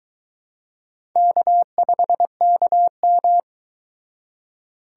Morse code, heard as a keyed tone, about 23 wpm, 700 Hz. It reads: K5KM